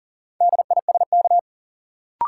Morse code, heard as a keyed tone, about 40 wpm, 700 Hz. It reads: DISK
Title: Sentences from Top 1000 Words - Review Part 6 - New Word Drill - 40wpm